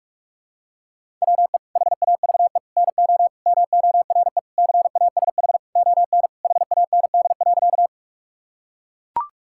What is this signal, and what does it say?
Morse code, heard as a keyed tone, about 42 wpm, 700 Hz. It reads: WE HAVE NO MORE CASH ON HAND.
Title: Sentences from Top 1000 Words - Review Part 5 - 45wpm